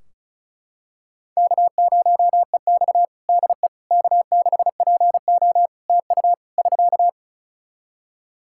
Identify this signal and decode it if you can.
Morse code, heard as a keyed tone, about 34 wpm, 700 Hz. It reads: K0EX DE K6PO TU <SK>
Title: Practice QSO - Exchange 4 - Ending - 35wpm